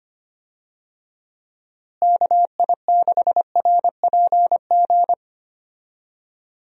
Morse code, heard as a keyed tone, about 25 wpm, 700 Hz. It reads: KI6RPG
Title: USA Call Signs - 6 Characters - Repeated 2x - 25wpm